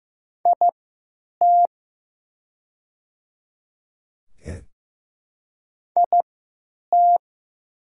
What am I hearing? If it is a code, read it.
Morse code, heard as a keyed tone, about 15 wpm, spaced as for 9 wpm, 700 Hz. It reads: IT IT
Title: 3x Character Spacing - Top 300 Words - 15wpm